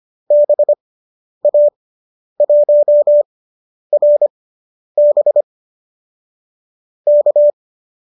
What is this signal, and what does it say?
Morse code, heard as a keyed tone, about 25 wpm, spaced as for 10 wpm, 590 Hz. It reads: BA1RB K